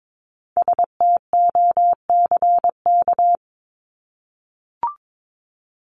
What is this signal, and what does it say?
Morse code, heard as a keyed tone, about 22 wpm, 700 Hz. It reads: STOCK
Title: Top 501-750 Words 22wpm